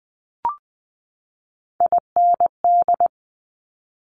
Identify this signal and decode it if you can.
Morse code, heard as a keyed tone, about 20 wpm, 700 Hz. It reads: IND
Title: Multi-letter Phonemes 20wpm